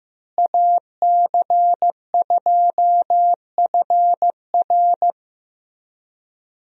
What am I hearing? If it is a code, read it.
Morse code, heard as a keyed tone, about 14 wpm, 700 Hz. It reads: AC2FR
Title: USA Call Signs - 5 Characters - Repeated 3x - 15wpm